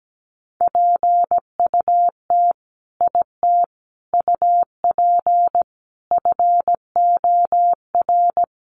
Morse code, heard as a keyed tone, about 17 wpm, 700 Hz. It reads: PUT IT UP FOR
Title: Sentences from Top 1400 Words 17wpm